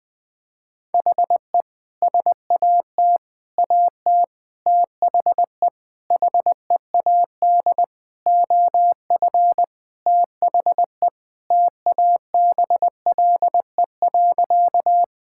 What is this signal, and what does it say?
Morse code, heard as a keyed tone, about 20 wpm, 700 Hz. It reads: HE SAT AT THE HEAD OF THE TABLE.
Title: Sentences with 5-Word Phrases - 20wpm